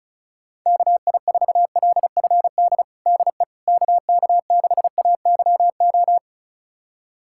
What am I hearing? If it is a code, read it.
Morse code, heard as a keyed tone, about 34 wpm, 700 Hz. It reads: KI4LFD DE KK6AYO
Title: USA Callsign DE Callsign 35wpm